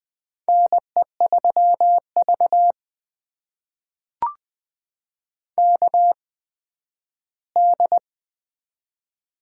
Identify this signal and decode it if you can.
Morse code, heard as a keyed tone, about 20 wpm, spaced as for 12 wpm, 700 Hz. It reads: NE3V KD
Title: USA Call Signs - 8x Character Spacing - 20wpm